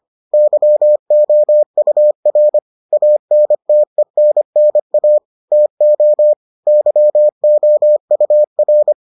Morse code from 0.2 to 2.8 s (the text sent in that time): YOUR